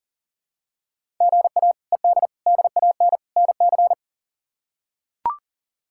Morse code, heard as a keyed tone, about 40 wpm, 700 Hz. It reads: GA ED DAN NC
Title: K1USN SST QSO Practice - 40wpm